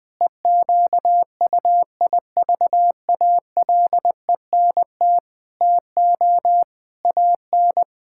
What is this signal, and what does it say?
Morse code, heard as a keyed tone, about 20 wpm, 700 Hz. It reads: EQUIVALENT TO AN
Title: Sentences from Top 1600 Words 20wpm